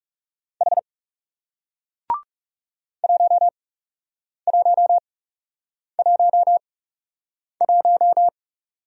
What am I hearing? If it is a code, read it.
Morse code, heard as a keyed tone, about 40 wpm, spaced as for 8 wpm, 700 Hz. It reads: H 1111